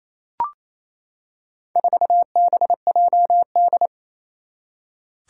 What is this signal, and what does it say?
Morse code, heard as a keyed tone, about 28 wpm, 700 Hz. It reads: 4BJD
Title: Four Challenging Letter-Number Combinations 28wpm